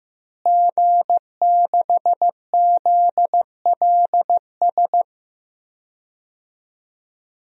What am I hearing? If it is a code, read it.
Morse code, heard as a keyed tone, about 15 wpm, 700 Hz. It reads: G6ZLS
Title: DXCC-100 Call Signs - 5 Characters - Repeated 3x - 15wpm